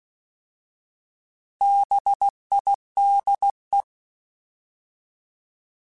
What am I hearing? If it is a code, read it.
Morse code, heard as a keyed tone, about 16 wpm, 800 Hz. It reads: BIDE
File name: Alphabetical / Alphabetical mp3s / Bs @15